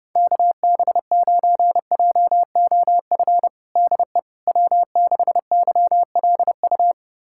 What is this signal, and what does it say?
Morse code, heard as a keyed tone, about 32 wpm, 700 Hz. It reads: KB9JOF DE W6YLU
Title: USA Callsign DE Callsign 30wpm